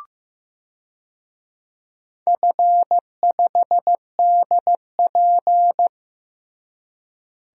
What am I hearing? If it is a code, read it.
Morse code, heard as a keyed tone, about 15 wpm, 700 Hz. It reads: F5DP
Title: DXCC-100 Call Signs - 4 Characters - Repeated 3x - 15wpm